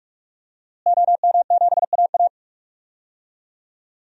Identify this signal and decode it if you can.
Morse code, heard as a keyed tone, about 46 wpm, 700 Hz. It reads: OM7AA
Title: DXCC-100 Call Signs - 5 Characters - Repeated 3x - 45wpm